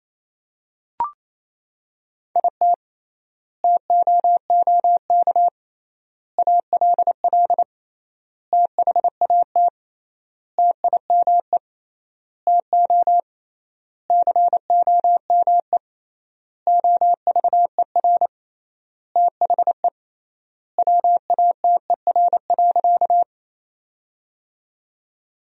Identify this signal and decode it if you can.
Morse code, heard as a keyed tone, about 28 wpm, 700 Hz. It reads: IT TOOK ALL THAT TIME TO COME OVER THE WATER.
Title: 3x Word-Spacing - Sentences from Top 300 Words - 28wpm